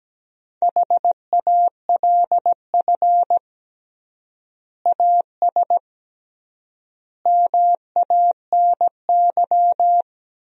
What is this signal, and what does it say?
Morse code, heard as a keyed tone, about 17 wpm, 700 Hz. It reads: HALF AS MANY
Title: 3x Word-Spacing - Sentences from Top 400 Words - 17wpm